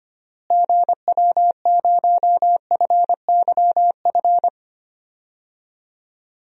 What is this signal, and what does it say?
Morse code, heard as a keyed tone, about 25 wpm, 700 Hz. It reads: GW0FYF
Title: DXCC-100 Call Signs - 6 Characters - Repeated 3x - 25wpm